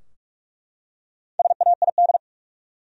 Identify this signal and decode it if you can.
Morse code, heard as a keyed tone, about 43 wpm, 700 Hz. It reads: SAID
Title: Top 100 Words Repeated 45wpm